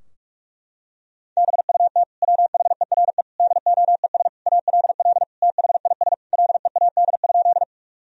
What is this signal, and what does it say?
Morse code, heard as a keyed tone, about 46 wpm, 700 Hz. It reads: BUT WHERE DOES ALL THIS LEAD?
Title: Sentences from Top 1000 Words - Review Part 2 - 45wpm